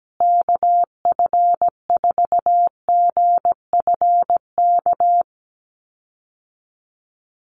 Morse code, heard as a keyed tone, about 17 wpm, 700 Hz. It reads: KF4GFK
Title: USA Call Signs Repeated 2x 17wpm